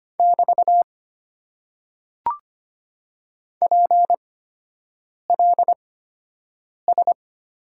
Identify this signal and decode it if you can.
Morse code, heard as a keyed tone, about 25 wpm, spaced as for 7 wpm, 700 Hz. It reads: <BT> PLS
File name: Top 90 QSO Elements - 8x Character Spacing - 25wpm